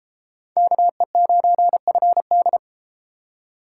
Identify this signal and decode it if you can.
Morse code, heard as a keyed tone, about 32 wpm, 700 Hz. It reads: KE9FD